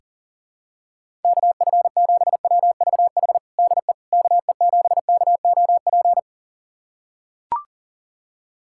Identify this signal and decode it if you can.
Morse code, heard as a keyed tone, about 40 wpm, 700 Hz. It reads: KF7WVH DE KE7KOP
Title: USA Callsign DE Callsign - 6 Characters - 40wpm